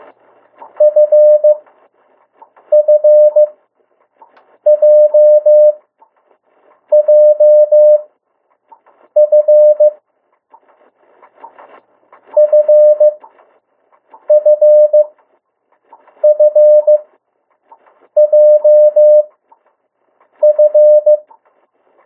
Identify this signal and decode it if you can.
Morse code, heard as a keyed tone, about 15 wpm, spaced as for 6 wpm, 600 Hz. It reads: FFJJF FFFJF
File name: urok 1